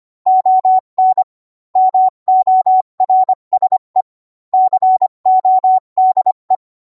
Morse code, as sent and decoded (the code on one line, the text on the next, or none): --- -. / -- --- .-. ... / -.-. --- -.. .
ON MORSE CODE